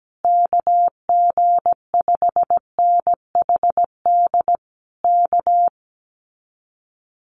Morse code, heard as a keyed tone, about 17 wpm, 700 Hz. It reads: KG5NHD K
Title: Common QSO Phrases Encore 17wpm